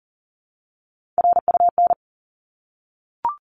Morse code, heard as a keyed tone, about 40 wpm, 700 Hz. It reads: RUN
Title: Top 300 Words 40wpm